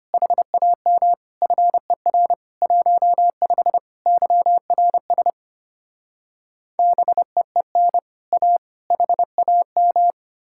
Morse code, heard as a keyed tone, about 33 wpm, 700 Hz. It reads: HAM FER 15 YRS BEEN A HAM